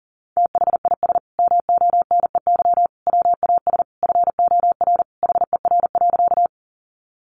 Morse code, heard as a keyed tone, about 40 wpm, 700 Hz. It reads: THIS MONEY WAS FOR HER.
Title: Sentences from Top 400 Words 40wpm